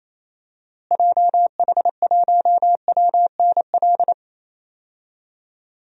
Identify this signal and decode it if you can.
Morse code, heard as a keyed tone, about 28 wpm, 700 Hz. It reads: JH1WNL
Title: DXCC-100 Call Signs - 6 Characters - Repeated 3x - 28wpm